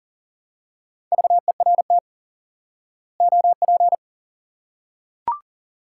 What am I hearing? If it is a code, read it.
Morse code, heard as a keyed tone, about 39 wpm, 700 Hz. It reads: VERT OP
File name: IQE Course 3 - Lesson 89 - YAGI - Two Elements - 40wpm